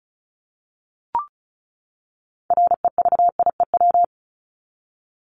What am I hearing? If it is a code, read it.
Morse code, heard as a keyed tone, about 34 wpm, 700 Hz. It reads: REVIEW
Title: Top 751-1000 Words 35wpm